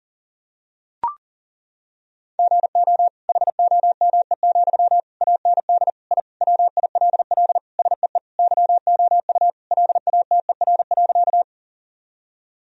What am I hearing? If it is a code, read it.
Morse code, heard as a keyed tone, about 40 wpm, 700 Hz. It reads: GO HOME, AND I WILL SEE YOU LATER.
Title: Sentences from Top 1000 Words - Review Part 4 - 40wpm